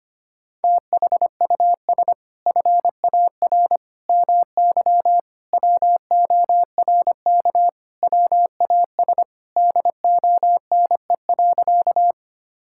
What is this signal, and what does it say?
Morse code, heard as a keyed tone, about 25 wpm, 700 Hz. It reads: THUS FAR MY WORK WAS DONE.
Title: Sentences from Top 1000 Words - Review Part 5 - 25wpm